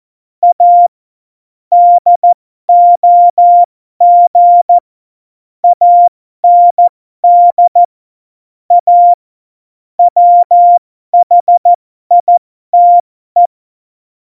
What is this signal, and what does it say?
Morse code, heard as a keyed tone, about 14 wpm, spaced as for 12 wpm, 700 Hz. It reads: A DOG AND A WHITE
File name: Morsekurs / Witze